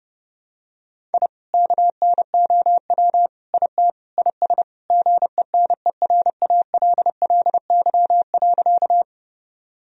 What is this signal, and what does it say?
Morse code, heard as a keyed tone, about 30 wpm, 700 Hz. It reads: I KNOW IT IS GENERALLY.
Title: Sentences from Top 1000 Words - Review Part 4 - 30wpm